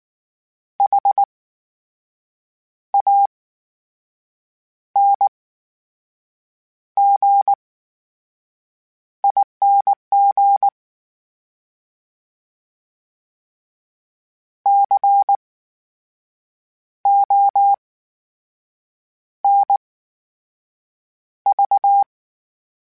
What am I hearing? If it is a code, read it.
Morse code, heard as a keyed tone, about 19 wpm, spaced as for 6 wpm, 800 Hz. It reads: HANGING CONV
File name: Special INGs / Special INGs-mp3s / VerySlowToFaster Ing @18